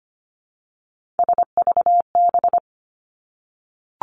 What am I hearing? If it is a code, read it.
Morse code, heard as a keyed tone, about 25 wpm, 700 Hz. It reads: SVB
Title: Three Challenging Letter-Number Combinations 25wpm